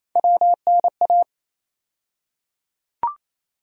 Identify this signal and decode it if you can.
Morse code, heard as a keyed tone, about 28 wpm, 700 Hz. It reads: WNA
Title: USA Call Sign Postfixes - 3 Letters 28wpm